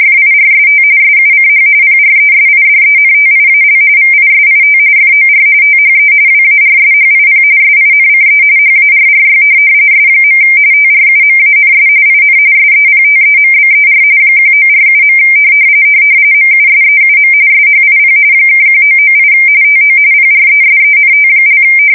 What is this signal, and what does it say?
RTTY, about 45.45 baud. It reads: FIRST SUCH PRESIDENTIAL VISIT FOR THE WARSHIP SINCE HARRY S. TRUMANS IN SEPTEMBER 1947. MUSEUM SHIP (1998 TO PRESENT)
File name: fsk170 / USS-Missouri